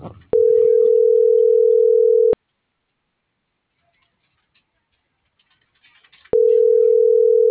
Ringback tone (North America), 440 + 480 Hz.